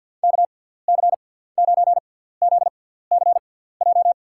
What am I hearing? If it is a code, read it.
Morse code, heard as a keyed tone, about 53 wpm, spaced as for 17 wpm, 700 Hz. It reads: KC9ZCJ
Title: USA Call Signs - 6 Characters - 6x Character Spacing - 50wpm